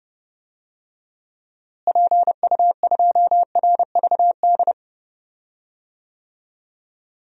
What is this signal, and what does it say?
Morse code, heard as a keyed tone, about 30 wpm, 700 Hz. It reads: PU2RVD
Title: DXCC-100 Call Signs - 6 Characters - Repeated 2x - 30wpm